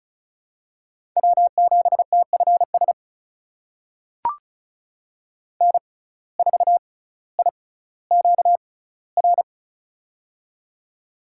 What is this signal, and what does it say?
Morse code, heard as a keyed tone, about 36 wpm, spaced as for 18 wpm, 700 Hz. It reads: W7TFS N4IQR
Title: USA Call Signs - 6x Character Spacing - 35wpm